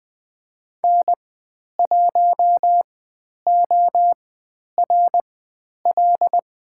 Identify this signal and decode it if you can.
Morse code, heard as a keyed tone, about 20 wpm, spaced as for 10 wpm, 700 Hz. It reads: N1ORL
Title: 095 Lesson 16g - Letter c - Call Signs 20wpm